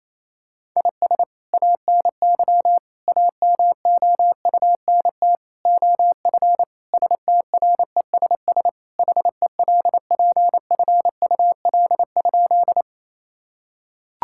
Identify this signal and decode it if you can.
Morse code, heard as a keyed tone, about 28 wpm, 700 Hz. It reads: IS ANY AMOUNT OF STRESS HELPFUL?
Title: Sentences from Top 1000 Words - Review Part 6 - 28wpm